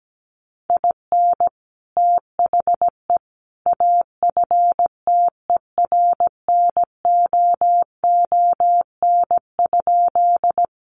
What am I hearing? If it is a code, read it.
Morse code, heard as a keyed tone, about 17 wpm, 700 Hz. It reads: IN THE AFTERNOON?
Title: Sentences from Top 1500 Words 17wpm